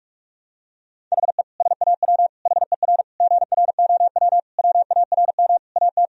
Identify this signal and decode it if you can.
Morse code, heard as a keyed tone, about 46 wpm, 700 Hz. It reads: HE SAW HER GROW WARM AT
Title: Sentences from Top 1000 Words - Review Part 2 - 45wpm